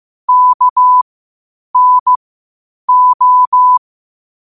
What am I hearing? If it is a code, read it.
Morse code, heard as a keyed tone, about 15 wpm, spaced as for 8 wpm, 1000 Hz. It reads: KNO
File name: qotdfiles / 8wpm